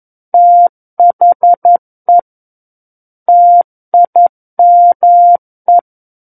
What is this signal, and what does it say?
Morse code, heard as a keyed tone, about 11 wpm, 700 Hz. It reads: THE TIME
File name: Book One / 02-Chapter 2 - 11 WPM